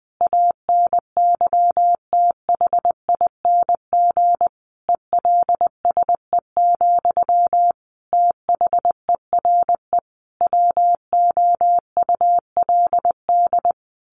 Morse code, heard as a keyed tone, about 20 wpm, 700 Hz. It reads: ANYTHING ELSE, THERE WOULD
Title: Sentences from Top 1200 Words 20wpm